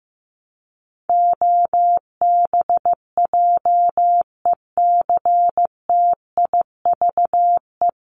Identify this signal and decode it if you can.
Morse code, heard as a keyed tone, about 15 wpm, 700 Hz. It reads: OBJECTIVE
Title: Top 1001-1250 Words 15wpm